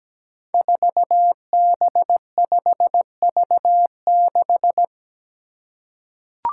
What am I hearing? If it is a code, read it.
Morse code, heard as a keyed tone, about 17 wpm, 700 Hz. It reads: 4B5V6
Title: Five Challenging Letter-Number Combinations 17wpm